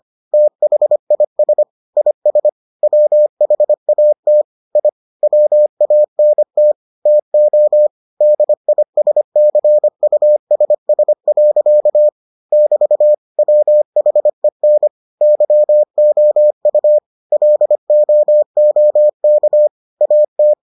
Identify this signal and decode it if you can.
Morse code, heard as a keyed tone, about 25 wpm, 600 Hz. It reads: THIS IS WHAT I WANT TO DISCUSS. <BT> WHEN YOU LOOK AT